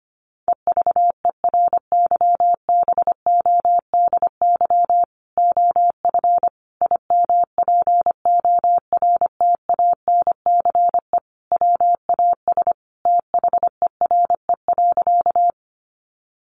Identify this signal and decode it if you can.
Morse code, heard as a keyed tone, about 25 wpm, 700 Hz. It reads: EVERYBODY OF IMPORTANCE WAS THERE.
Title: Sentences from Top 1400 Words 25wpm